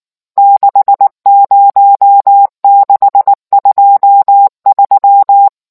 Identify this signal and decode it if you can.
Morse code, heard as a keyed tone, about 19 wpm, 800 Hz. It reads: 60623